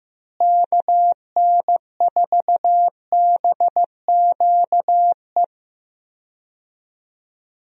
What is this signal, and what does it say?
Morse code, heard as a keyed tone, about 15 wpm, 700 Hz. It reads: KN4BQE
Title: USA Call Signs - 6 Characters - Repeated 2x - 15wpm